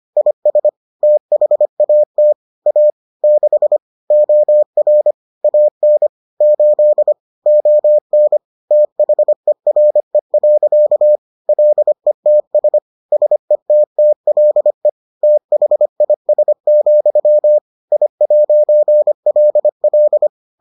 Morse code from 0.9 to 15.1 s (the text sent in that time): THAT A 6 OR AN 8 ON THERE. LETS SETTLE